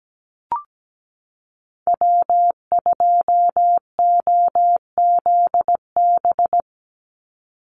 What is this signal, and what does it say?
Morse code, heard as a keyed tone, about 17 wpm, 700 Hz. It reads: W2OZB
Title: Call Signs Repeated 3x - 17wpm